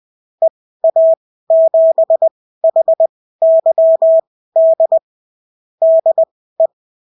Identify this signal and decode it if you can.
Morse code, heard as a keyed tone, about 20 wpm, spaced as for 14 wpm, 650 Hz. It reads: EA7HYD DE
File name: CW QSO libroCW4 20-10ppm